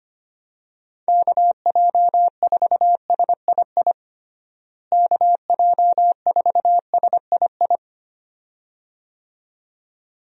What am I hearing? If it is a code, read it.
Morse code, heard as a keyed tone, about 25 wpm, 700 Hz. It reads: KJ4SII KJ4SII
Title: USA Call Signs - 6 Characters - Repeated 2x - 25wpm